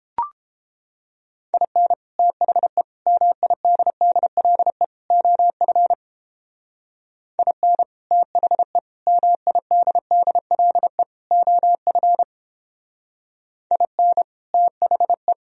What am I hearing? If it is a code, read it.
Morse code, heard as a keyed tone, about 31 wpm, 700 Hz. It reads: IN THE MIDDLE OF IN THE MIDDLE OF IN THE